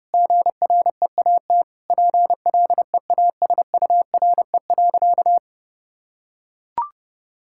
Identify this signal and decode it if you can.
Morse code, heard as a keyed tone, about 30 wpm, 700 Hz. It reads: GREAT PLEASURE.
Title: Sentences from Top 1600 Words 30wpm